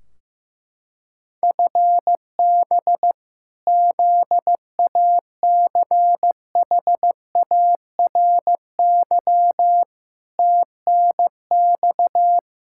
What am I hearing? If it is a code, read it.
Morse code, heard as a keyed tone, about 15 wpm, 700 Hz. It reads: FB ZACHARY TNX
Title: Practice QSO - Exchange 3 - More Info - 15wpm